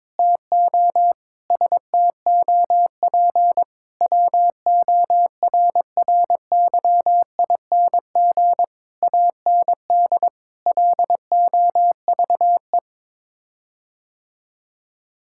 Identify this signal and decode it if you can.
Morse code, heard as a keyed tone, about 22 wpm, 700 Hz. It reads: TO STOP WORRYING AND LOVE
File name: practice / 22wpm / Sets of 8 Words - 22wpm